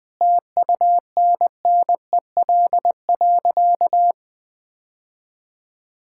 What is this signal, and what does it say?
Morse code, heard as a keyed tone, about 20 wpm, 700 Hz. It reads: TUNNEL.
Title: Sentences with 8-Word Phrases - Spoken Beginning - 20wpm